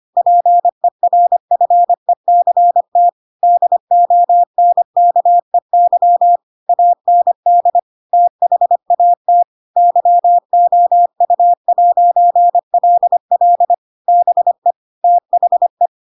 Morse code, heard as a keyed tone, about 25 wpm, 700 Hz. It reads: PERFECT DONKEY AND THAT YOU'LL BE THE
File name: Pinocchio / Pinocchio ch4 25 25 ik1hge